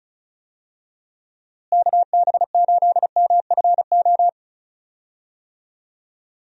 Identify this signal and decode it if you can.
Morse code, heard as a keyed tone, about 36 wpm, 700 Hz. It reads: KB8MFO
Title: USA Call Signs - 6 Characters - Repeated 2x - 35wpm